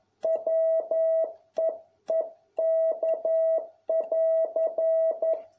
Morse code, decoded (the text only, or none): WEEK<AR>